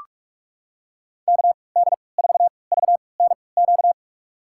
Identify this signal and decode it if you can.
Morse code, heard as a keyed tone, about 41 wpm, spaced as for 25 wpm, 700 Hz. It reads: KD4VNQ
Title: USA Call Signs - 6 Characters - 3x Character Spacing - 45wpm